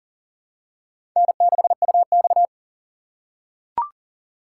Morse code, heard as a keyed tone, about 40 wpm, 700 Hz. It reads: N6UX
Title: USA Call Signs - 4 Characters - 2x Character Spacing - 40wpm